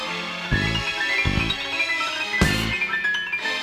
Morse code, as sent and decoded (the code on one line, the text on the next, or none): none